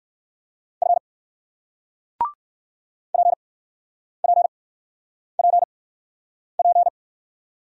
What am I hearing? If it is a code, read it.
Morse code, heard as a keyed tone, about 60 wpm, spaced as for 9 wpm, 700 Hz. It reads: F PPPP